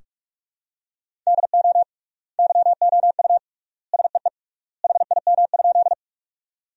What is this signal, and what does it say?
Morse code, heard as a keyed tone, about 43 wpm, 700 Hz. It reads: DO YOU SEE HIM?
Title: 3x Word-Spacing - Sentences from Top 100 Words - 45wpm